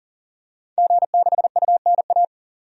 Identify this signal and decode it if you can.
Morse code, heard as a keyed tone, about 40 wpm, 700 Hz. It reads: G6UNA
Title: DXCC-100 Call Signs - 5 Characters - 40wpm